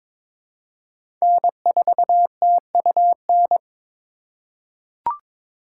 Morse code, heard as a keyed tone, about 22 wpm, 700 Hz. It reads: N4TUN
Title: USA Call Signs - 5 Characters - 3x Character Spacing - 22wpm